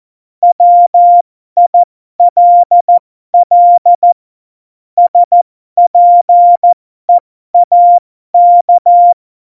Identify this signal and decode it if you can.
Morse code, heard as a keyed tone, about 14 wpm, spaced as for 12 wpm, 700 Hz. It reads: WILL SPEAK
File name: Morsekurs / Witze